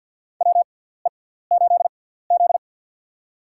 Morse code, heard as a keyed tone, about 47 wpm, spaced as for 17 wpm, 700 Hz. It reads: WE8Z
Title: USA Call Signs - 4 Characters - 6x Character Spacing - 50wpm